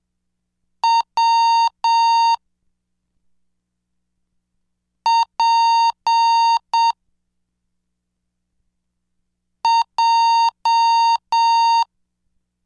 Morse code, heard as a keyed tone, about 7 wpm, 920 Hz. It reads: W P J
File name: cursos / CDestudo / 13 Track 13